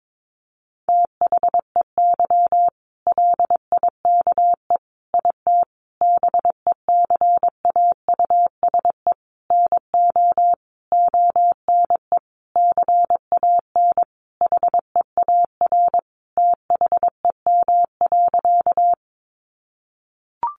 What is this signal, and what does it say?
Morse code, heard as a keyed tone, about 22 wpm, 700 Hz. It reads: THEY LIKE IT BECAUSE NO ONE CAN HEAR THEM.
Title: Sentences from Top 600 Words 22wpm